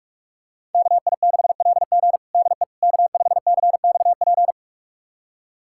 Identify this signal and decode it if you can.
Morse code, heard as a keyed tone, about 42 wpm, 700 Hz. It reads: KI6LG DE K5CXP
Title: USA Callsign DE Callsign - 5 Characters - 45wpm